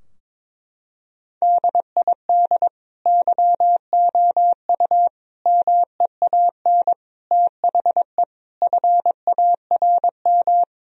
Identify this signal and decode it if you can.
Morse code, heard as a keyed tone, about 22 wpm, 700 Hz. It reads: DID YOU MEAN THE FARM
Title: Sentences from Top 1000 Words - Review Part 2 - 22wpm